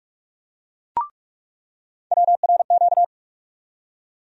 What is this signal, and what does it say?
Morse code, heard as a keyed tone, about 46 wpm, 700 Hz. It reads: WRQ
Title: USA Call Sign Postfixes - 3 Letters 45wpm